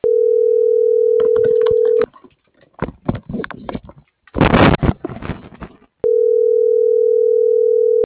Ringback tone (North America), 440 + 480 Hz.